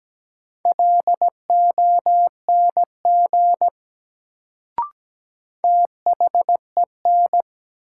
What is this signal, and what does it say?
Morse code, heard as a keyed tone, about 17 wpm, 700 Hz. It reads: LONG THEN
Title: Sentences from Top 100 Words - New Word Drill - 17wpm